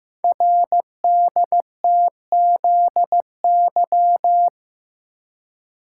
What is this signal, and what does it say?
Morse code, heard as a keyed tone, about 15 wpm, 690 Hz. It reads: RDTZY
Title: Code Groups - 5 Letters - 15wpm